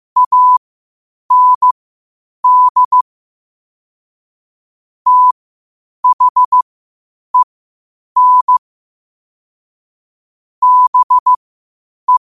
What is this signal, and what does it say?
Morse code, heard as a keyed tone, about 15 wpm, spaced as for 8 wpm, 1000 Hz. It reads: AND THEN BE
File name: 8wpm